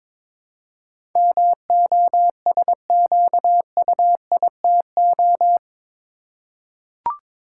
Morse code, heard as a keyed tone, about 22 wpm, 700 Hz. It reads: MOSQUITO
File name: practice / 22wpm / Eight Letter Words - 22wpm